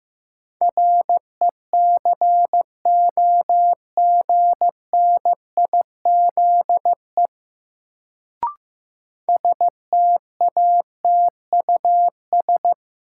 Morse code, heard as a keyed tone, about 15 wpm, 700 Hz. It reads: RECOGNIZE STATUS
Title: Sentences from Top 1400 Words - New Word Drill - 15wpm